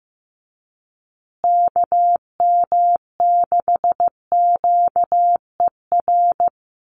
Morse code, heard as a keyed tone, about 15 wpm, 700 Hz. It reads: KM6QER
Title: USA Call Signs Repeated 3x 15wpm